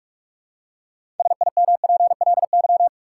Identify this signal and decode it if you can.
Morse code, heard as a keyed tone, about 44 wpm, 700 Hz. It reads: SIMPLY
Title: Six Letter Words - 45wpm